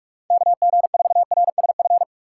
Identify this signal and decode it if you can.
Morse code, heard as a keyed tone, about 43 wpm, 700 Hz. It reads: KG4RSF